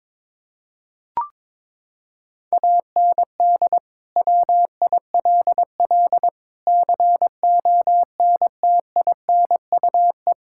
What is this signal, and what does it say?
Morse code, heard as a keyed tone, about 22 wpm, 700 Hz. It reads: AND WILL CONTINUE
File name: practice / 22wpm / Sets of 6 Words - 22wpm